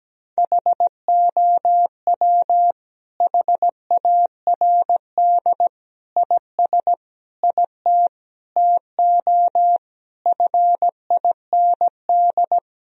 Morse code, heard as a keyed tone, about 17 wpm, 700 Hz. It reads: HOW HARD IS IT TO FIND
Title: Sentences from Top 1000 Words - Review Part 1 - 17wpm